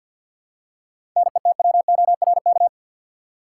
Morse code, heard as a keyed tone, about 53 wpm, 700 Hz. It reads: NETWORK
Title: Seven Letter Words - 50wpm